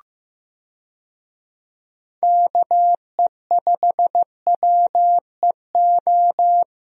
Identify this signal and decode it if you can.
Morse code, heard as a keyed tone, about 16 wpm, 700 Hz. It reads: KE5WEO
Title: USA Call Signs - 6 Characters - Repeated 2x - 15wpm